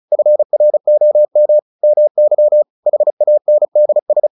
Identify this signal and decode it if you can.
Morse code, heard as a keyed tone, about 34 wpm, 600 Hz. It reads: FROM MY HANDS
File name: the old man and the sea 2 35wpm